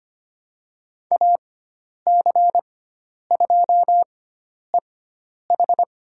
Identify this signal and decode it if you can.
Morse code, heard as a keyed tone, about 25 wpm, spaced as for 10 wpm, 700 Hz. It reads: AC2EH
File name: Lesson 16g - Letter c - Call Signs 25wpm